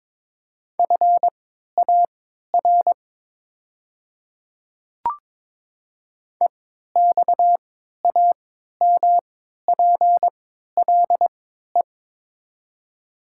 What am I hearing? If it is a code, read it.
Morse code, heard as a keyed tone, about 22 wpm, spaced as for 13 wpm, 700 Hz. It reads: FAR EXAMPLE
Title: 3x Character Spacing - Top 500 Words - 22wpm